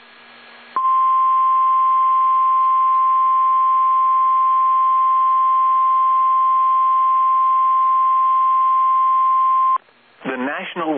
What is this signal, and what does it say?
Emergency Alert System (SAME) tones (the 1050 Hz attention tone), with speech. It is heard from 0.7 to 9.8 s.